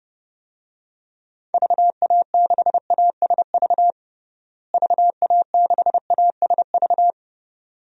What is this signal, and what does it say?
Morse code, heard as a keyed tone, about 31 wpm, 700 Hz. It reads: VA6ASV VA6ASV
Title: DXCC-100 Call Signs - 6 Characters - Repeated 2x - 30wpm